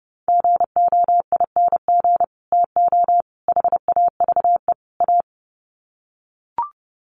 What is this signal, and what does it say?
Morse code, heard as a keyed tone, about 30 wpm, 700 Hz. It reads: GOING TO HAVE A
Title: Sets of 4 Words 30wpm